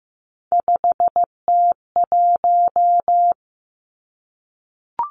Morse code, heard as a keyed tone, about 15 wpm, 700 Hz. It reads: 5T1
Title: Three Challenging Letter-Number Combinations 15wpm